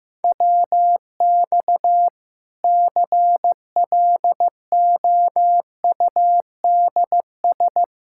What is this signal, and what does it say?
Morse code, heard as a keyed tone, about 15 wpm, 700 Hz. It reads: WX CLOUDS
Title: K1USN SST QSO Practice 15wpm